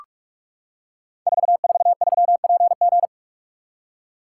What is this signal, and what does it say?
Morse code, heard as a keyed tone, about 44 wpm, 700 Hz. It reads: 443PG
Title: Code Groups - 5 Letter:Numbers - 45wpm